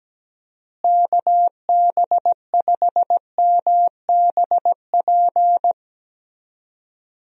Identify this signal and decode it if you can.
Morse code, heard as a keyed tone, about 17 wpm, 700 Hz. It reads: KB5MBP
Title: USA Call Signs - 6 Characters - Repeated 2x - 17wpm